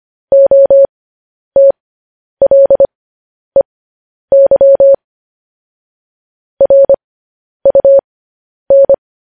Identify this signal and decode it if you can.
Morse code, heard as a keyed tone, about 25 wpm, spaced as for 10 wpm, 550 Hz. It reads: OTLEY RUN
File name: podcast / News headlines-1x-25x10